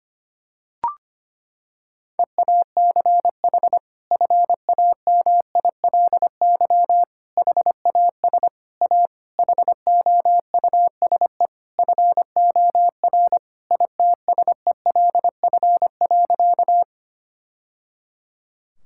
Morse code, heard as a keyed tone, about 25 wpm, 700 Hz. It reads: EACH FAMILY HAS A HOUSE FOR ITSELF.
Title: Sentences from Top 1000 Words - Review Part 5 - 25wpm